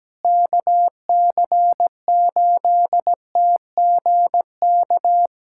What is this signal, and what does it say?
Morse code, heard as a keyed tone, about 17 wpm, 700 Hz. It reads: KC8TGK
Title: USA Callsign DE Callsign - 6 Characters - 17wpm